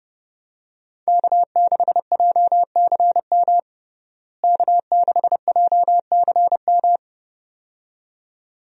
Morse code, heard as a keyed tone, about 31 wpm, 700 Hz. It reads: K6JCM K6JCM
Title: USA Call Signs - 5 Characters - Repeated 2x - 30wpm